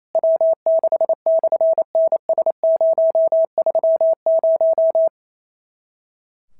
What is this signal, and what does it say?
Morse code, heard as a keyed tone, about 28 wpm, 650 Hz. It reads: W6/NS030
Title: SOTA QSO Practice 28wpm